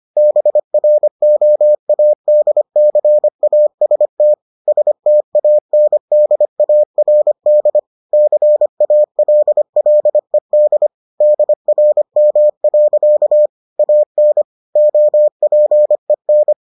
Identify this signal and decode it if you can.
Morse code, heard as a keyed tone, about 25 wpm, 600 Hz. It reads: BROADCAST STANDARD CALLED DRM. AN OPEN